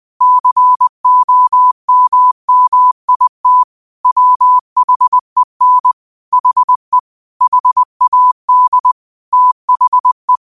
Morse code, heard as a keyed tone, about 20 wpm, 1000 Hz. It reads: COMMIT WHEN HE HAD THE
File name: qotdfiles / 20wpm